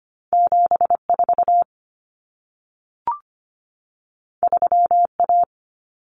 Morse code, heard as a keyed tone, about 25 wpm, 700 Hz. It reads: 74 3A
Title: Common Cut Numbers - 2-Digit 25wpm